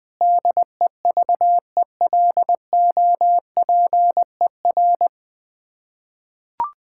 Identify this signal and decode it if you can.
Morse code, heard as a keyed tone, about 20 wpm, 700 Hz. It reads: DEVELOPER
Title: K1USN SST QSO Practice 20wpm